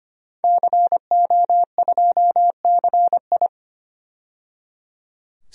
Morse code, heard as a keyed tone, about 25 wpm, 700 Hz. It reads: CO2CI
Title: DXCC-100 Call Signs - 5 Characters - 25wpm